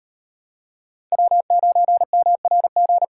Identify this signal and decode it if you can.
Morse code, heard as a keyed tone, about 38 wpm, 700 Hz. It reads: W9MRG